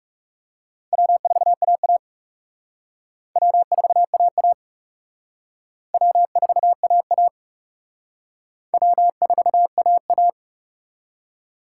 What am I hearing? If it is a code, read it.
Morse code, heard as a keyed tone, about 37 wpm, 700 Hz. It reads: W4AA W4AA W4AA W4AA
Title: Speed Racing - USA 1x2 Call Signs 30wpm